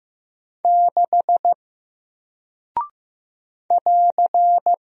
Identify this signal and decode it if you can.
Morse code, heard as a keyed tone, about 15 wpm, 700 Hz. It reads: 6 <AR>
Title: Single Character (Letter:Number:Punctuation:Prosign) Alt 15wpm